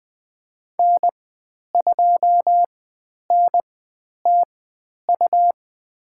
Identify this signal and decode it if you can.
Morse code, heard as a keyed tone, about 20 wpm, spaced as for 10 wpm, 700 Hz. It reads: N2NTU
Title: 109 Lesson 18g - Letter m - Call Signs 20wpm